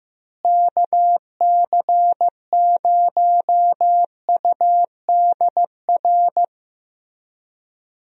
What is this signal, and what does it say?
Morse code, heard as a keyed tone, about 16 wpm, 700 Hz. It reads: KC0UDR
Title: USA Call Signs - 6 Characters - Repeated 2x - 15wpm